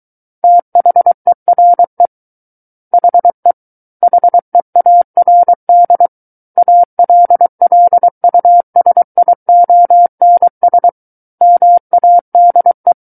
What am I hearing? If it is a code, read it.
Morse code, heard as a keyed tone, about 23 wpm, 700 Hz. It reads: THERE HE HEARD ALLUSIONS MADE
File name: Book One / 14-Chapter 14 - 23 WPM